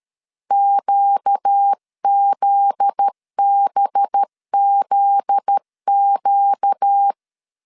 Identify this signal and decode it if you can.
Morse code, heard as a keyed tone, about 13 wpm, 800 Hz. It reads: QZBZQ
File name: LEZIONI / 7 GF9Z / 7 GF9Z 12wpm 2147